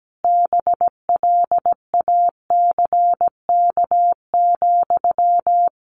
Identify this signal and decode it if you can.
Morse code, heard as a keyed tone, about 17 wpm, 700 Hz. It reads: BLACK,
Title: Sentences from Top 800 Words 17wpm